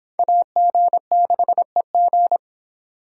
Morse code, heard as a keyed tone, about 23 wpm, 700 Hz. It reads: AG6EG